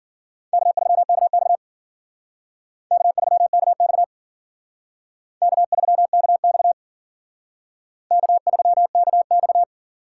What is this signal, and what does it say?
Morse code, heard as a keyed tone, about 52 wpm, 700 Hz. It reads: K3KX K3KX K3KX K3KX